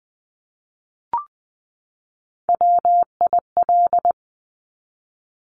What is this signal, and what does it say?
Morse code, heard as a keyed tone, about 20 wpm, 700 Hz. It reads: WIL
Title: Top Three Letter Combinations in Words 20wpm